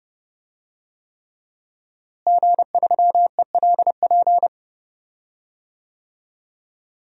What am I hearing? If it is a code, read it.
Morse code, heard as a keyed tone, about 30 wpm, 700 Hz. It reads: G3ELP
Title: DXCC-100 Call Signs - 5 Characters - Repeated 3x - 30wpm